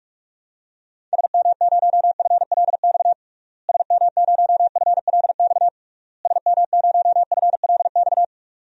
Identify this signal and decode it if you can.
Morse code, heard as a keyed tone, about 42 wpm, 700 Hz. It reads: SM0FLX SM0FLX SM0FLX
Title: DXCC-100 Call Signs - 6 Characters - Repeated 3x - 45wpm